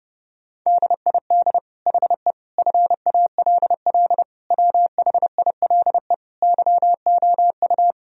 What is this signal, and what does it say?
Morse code, heard as a keyed tone, about 30 wpm, 700 Hz. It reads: DID HE FALL WHILE YOU
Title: Sentences from Top 1000 Words - Review Part 2 - 30wpm